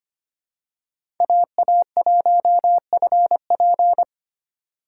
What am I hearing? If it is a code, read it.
Morse code, heard as a keyed tone, about 25 wpm, 700 Hz. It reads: AA1FP
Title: USA Call Signs - 5 Characters - 4x Character Spacing - 25wpm